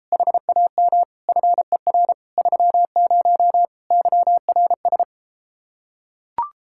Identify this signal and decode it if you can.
Morse code, heard as a keyed tone, about 33 wpm, 700 Hz. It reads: HAM FER 30 YRS